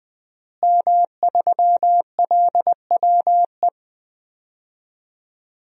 Morse code, heard as a keyed tone, about 21 wpm, 700 Hz. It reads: M3LWE
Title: DXCC-100 Call Signs - 5 Characters - Repeated 2x - 20wpm